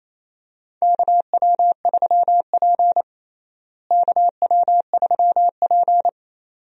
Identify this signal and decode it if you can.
Morse code, heard as a keyed tone, about 28 wpm, 700 Hz. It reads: KW3P KW3P
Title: USA Call Signs - 4 Characters - Repeated 2x - 28wpm